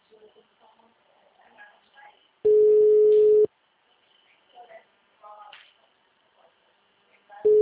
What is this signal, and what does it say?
Ringback tone (Europe, 425 Hz), with speech.